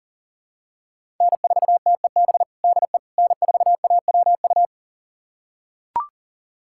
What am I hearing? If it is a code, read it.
Morse code, heard as a keyed tone, about 40 wpm, 700 Hz. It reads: N4TEB DE N4AWU
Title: USA Callsign DE Callsign - 5 Characters - 40wpm